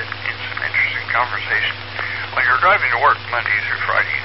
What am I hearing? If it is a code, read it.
Emergency Alert System (SAME) tones, heard as harsh buzzing bursts, with speech. It is heard from 0.6 to 3.8 s.